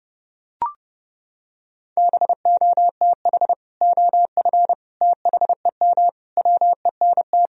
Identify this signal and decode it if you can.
Morse code, heard as a keyed tone, about 30 wpm, 700 Hz. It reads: BOTH OF THEM WENT
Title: Sentences from Top 1000 Words - Review Part 1 - 30wpm